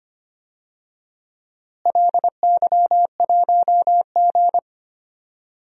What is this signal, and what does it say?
Morse code, heard as a keyed tone, about 25 wpm, 700 Hz. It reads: LY1G
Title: DXCC-100 Call Signs - 4 Characters - Repeated 3x - 25wpm